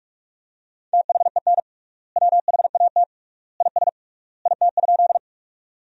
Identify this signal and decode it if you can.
Morse code, heard as a keyed tone, about 45 wpm, 700 Hz. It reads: THEN WHAT IS IT?
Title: 3x Word-Spacing - Sentences from Top 100 Words - 45wpm